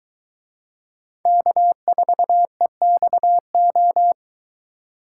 Morse code, heard as a keyed tone, about 22 wpm, 700 Hz. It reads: K4EXO